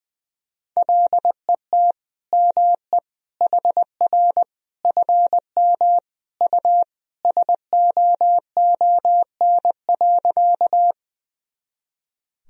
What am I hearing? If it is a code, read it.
Morse code, heard as a keyed tone, about 20 wpm, 700 Hz. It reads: LET ME HR FM U SOON.
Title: Sentences from Top 400 Words and Abbreviations 20wpm